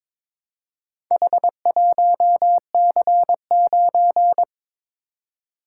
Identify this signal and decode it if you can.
Morse code, heard as a keyed tone, about 22 wpm, 700 Hz. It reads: H1C9
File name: practice / 22wpm / Four Letter-Number Combinations - 22wpm